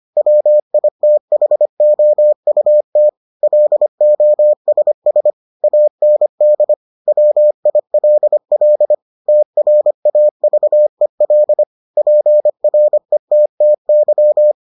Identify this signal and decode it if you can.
Morse code, heard as a keyed tone, about 25 wpm, 600 Hz. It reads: WITHOUT LOSS AND WILL TRAVEL PRETTY